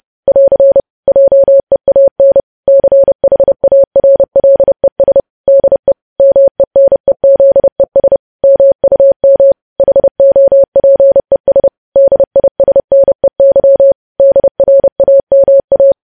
Morse code, heard as a keyed tone, about 30 wpm, 550 Hz. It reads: <AR> JEAN CHARLES DE MENEZES MUM HOPES DISNEY DRAMA
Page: news headlines in morse code at 30 wpm